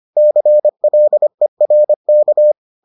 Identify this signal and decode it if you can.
Morse code, heard as a keyed tone, about 25 wpm, 600 Hz. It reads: CLERK